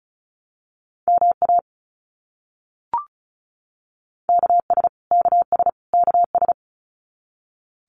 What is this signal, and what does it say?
Morse code, heard as a keyed tone, about 34 wpm, 700 Hz. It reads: MA KS KS KS
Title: US States Repeated 3x 35wpm